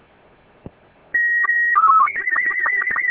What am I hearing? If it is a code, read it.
SSTV (slow-scan television), Robot 36.